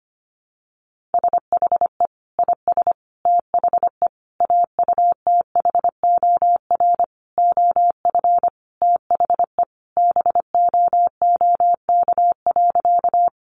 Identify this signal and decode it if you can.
Morse code, heard as a keyed tone, about 25 wpm, 700 Hz. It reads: SHE IS THE AUTHOR OF THE BOOK.
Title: Sentences from Top 1200 Words 25wpm